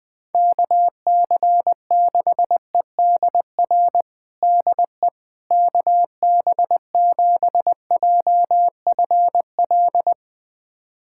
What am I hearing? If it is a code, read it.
Morse code, heard as a keyed tone, about 20 wpm, 700 Hz. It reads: KC6EDR DE KB7JFL
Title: USA Callsign DE Callsign 20wpm